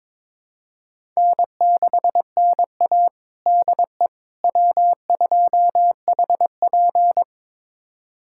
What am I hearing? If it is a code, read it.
Morse code, heard as a keyed tone, about 23 wpm, 700 Hz. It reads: N6NA DE W2HP
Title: USA Callsign DE Callsign - 4 Characters - 22wpm